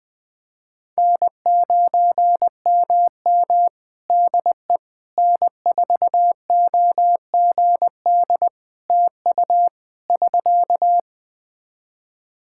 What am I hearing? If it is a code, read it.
Morse code, heard as a keyed tone, about 20 wpm, 700 Hz. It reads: N9MM DE N4OGD TU <SK>
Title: K1USN SST QSO Practice 20wpm